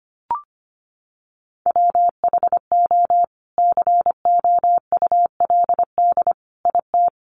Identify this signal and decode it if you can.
Morse code, heard as a keyed tone, about 25 wpm, 700 Hz. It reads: WHO COULD IT
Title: Sentences from Top 1300 Words 25wpm